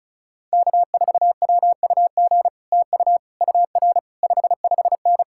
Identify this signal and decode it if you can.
Morse code, heard as a keyed tone, about 46 wpm, spaced as for 41 wpm, 700 Hz. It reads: K4WUG TU UR 55N
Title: Speed-Racing Common QSO Phrases Encore - 35wpm